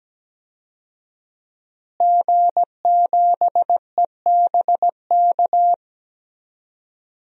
Morse code, heard as a keyed tone, about 17 wpm, 700 Hz. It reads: G7EBK
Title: DXCC-100 Call Signs - 5 Characters - Repeated 2x - 17wpm